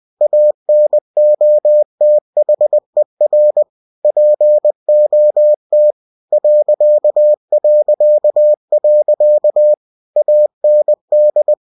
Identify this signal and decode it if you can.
Morse code, heard as a keyed tone, about 20 wpm, 600 Hz. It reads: ANOTHER POT ... AND
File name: cw-books / winnie-the-pooh / winnie-20wpm-page-0094